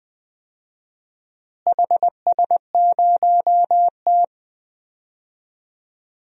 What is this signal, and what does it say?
Morse code, heard as a keyed tone, about 20 wpm, 700 Hz. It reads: HS0T
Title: DXCC-100 Call Signs - 4 Characters - Repeated 3x - 20wpm